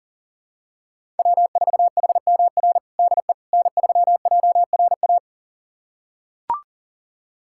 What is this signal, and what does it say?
Morse code, heard as a keyed tone, about 37 wpm, 700 Hz. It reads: W4HMR DE N3JRA
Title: USA Callsign DE Callsign - 5 Characters - 40wpm